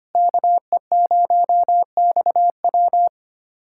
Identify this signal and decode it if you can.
Morse code, heard as a keyed tone, about 25 wpm, 700 Hz. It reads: KE0XW
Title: USA Callsign DE Callsign - 5 Characters - 25wpm